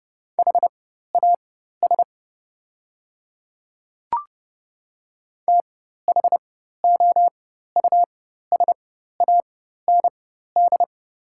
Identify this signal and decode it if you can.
Morse code, heard as a keyed tone, about 30 wpm, spaced as for 14 wpm, 700 Hz. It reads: HAS THOUSAND
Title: 4x Character Spacing - Top 500 Words - 30wpm